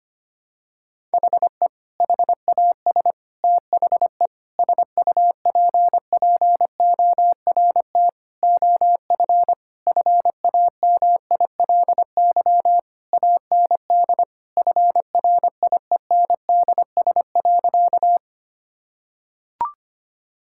Morse code, heard as a keyed tone, about 25 wpm, 700 Hz. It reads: HE HAS THE SUPPORT OF FAMILY AND FRIENDS.
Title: Sentences from Top 1000 Words - Review Part 5 - 25wpm